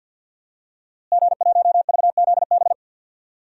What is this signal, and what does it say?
Morse code, heard as a keyed tone, about 53 wpm, 700 Hz. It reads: G1VZB
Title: DXCC-100 Call Signs - 5 Characters - Repeated 3x - 50wpm